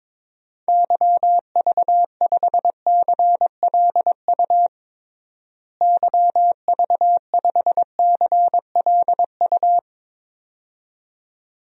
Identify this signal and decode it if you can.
Morse code, heard as a keyed tone, about 22 wpm, 700 Hz. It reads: YV5CLU YV5CLU
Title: DXCC-100 Call Signs - 6 Characters - Repeated 2x - 22wpm